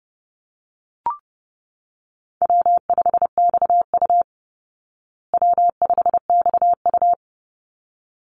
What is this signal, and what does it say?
Morse code, heard as a keyed tone, about 30 wpm, 700 Hz. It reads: W5XU W5XU
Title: Easy Call Signs Repeated 2x - 30wpm